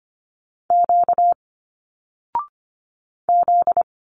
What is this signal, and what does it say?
Morse code, heard as a keyed tone, about 25 wpm, 700 Hz. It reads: Q Z
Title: Single Letters 25wpm